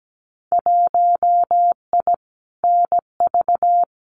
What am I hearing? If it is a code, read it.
Morse code, heard as a keyed tone, about 17 wpm, 700 Hz. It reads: 1I NV
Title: Winter Field Day Exchanges Encore 17wpm